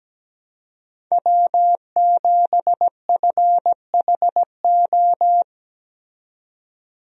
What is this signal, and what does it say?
Morse code, heard as a keyed tone, about 17 wpm, 700 Hz. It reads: W7FHO
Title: USA Call Signs - 5 Characters - 17wpm